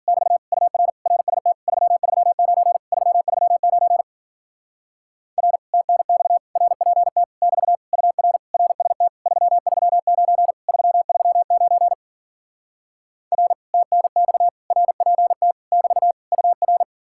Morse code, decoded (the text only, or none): <BT> UR RST 339 339 R TNX RPT <BT> UR RST 339 339 R TNX RPT <BT> UR